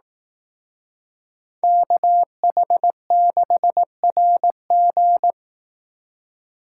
Morse code, heard as a keyed tone, about 18 wpm, 700 Hz. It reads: KH6RG
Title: Speed Racing - USA Call Signs 15wpm